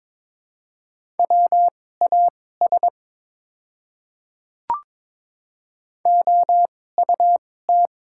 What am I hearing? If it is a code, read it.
Morse code, heard as a keyed tone, about 22 wpm, spaced as for 16 wpm, 690 Hz. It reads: WAS OUT
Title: 2x Character Spacing - Top 100 Words - 22wpm